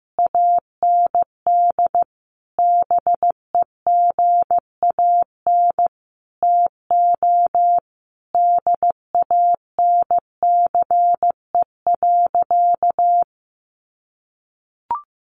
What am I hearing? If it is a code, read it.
Morse code, heard as a keyed tone, about 15 wpm, 700 Hz. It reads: AND BEGAN TO DANCE.
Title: Sentences from Top 900 Words 15wpm